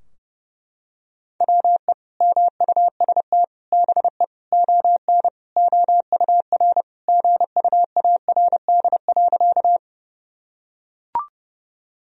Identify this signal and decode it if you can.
Morse code, heard as a keyed tone, about 30 wpm, 700 Hz. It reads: WE MUST BE ON OUR GUARD.
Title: Sentences from Top 1000 Words - Review Part 4 - 30wpm